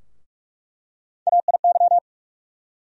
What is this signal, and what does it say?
Morse code, heard as a keyed tone, about 41 wpm, 700 Hz. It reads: AIY
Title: USA Call Sign Postfixes - 3 Letters 45wpm